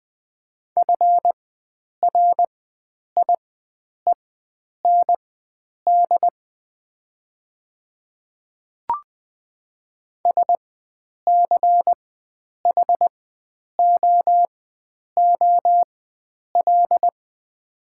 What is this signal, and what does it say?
Morse code, heard as a keyed tone, about 20 wpm, spaced as for 9 wpm, 700 Hz. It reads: FRIEND SCHOOL
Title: 4x Character Spacing - Top 300 Words - 20wpm